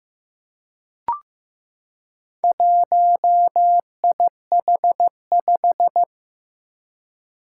Morse code, heard as a keyed tone, about 15 wpm, 700 Hz. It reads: 1IH5
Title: Four Challenging Letter-Number Combinations 15wpm